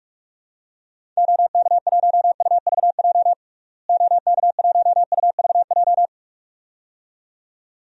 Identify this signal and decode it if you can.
Morse code, heard as a keyed tone, about 43 wpm, 700 Hz. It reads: OK1UVJ OK1UVJ
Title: DXCC-100 Call Signs - 6 Characters - Repeated 2x - 45wpm